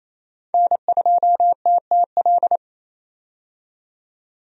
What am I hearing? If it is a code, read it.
Morse code, heard as a keyed tone, about 27 wpm, 700 Hz. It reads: N2TTL
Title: USA Call Signs - 5 Characters - 28wpm